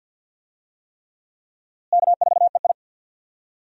Morse code, heard as a keyed tone, about 47 wpm, 700 Hz. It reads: K4EI
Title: USA Call Signs - 4 Characters - Repeated 3x - 50wpm